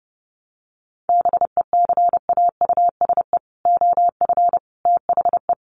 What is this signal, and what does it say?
Morse code, heard as a keyed tone, about 30 wpm, 700 Hz. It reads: BECAUSE OF THE
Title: Sets of 3 Words 30wpm